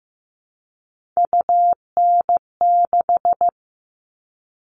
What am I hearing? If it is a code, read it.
Morse code, heard as a keyed tone, about 15 wpm, 700 Hz. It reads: UN6
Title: All Cut Numbers - 3-Digit 15wpm